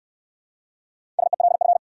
Morse code, heard as a keyed tone, about 69 wpm, 700 Hz. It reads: SELF